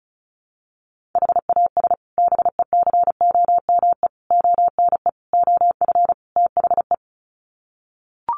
Morse code, heard as a keyed tone, about 34 wpm, 700 Hz. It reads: HAS BECOME ONE OF THE
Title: Sets of 5 Words Encore 35wpm